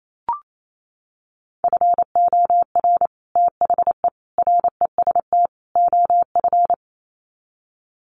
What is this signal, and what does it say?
Morse code, heard as a keyed tone, about 28 wpm, 700 Hz. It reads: FOR THE REST OF
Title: Sets of 4 Words 28wpm